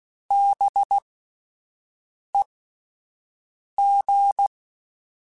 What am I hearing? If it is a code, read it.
Morse code, heard as a keyed tone, about 16 wpm, spaced as for 5 wpm, 800 Hz. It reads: BEG
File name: Spaced Alphas / Spaced Alphas mp3s / Bs sp @10